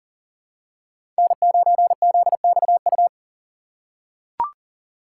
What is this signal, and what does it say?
Morse code, heard as a keyed tone, about 40 wpm, 700 Hz. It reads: N9ZXU
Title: USA Call Signs - 2x Character Spacing - 40wpm